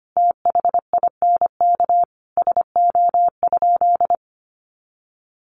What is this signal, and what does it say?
Morse code, heard as a keyed tone, about 25 wpm, 700 Hz. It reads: THINK SO?
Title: Sentences from Top 300 Words 25wpm